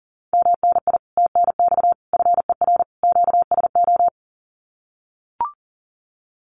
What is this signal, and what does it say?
Morse code, heard as a keyed tone, about 40 wpm, 700 Hz. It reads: MNI TNX FER QSO
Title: Common QSO Phrases Encore 40wpm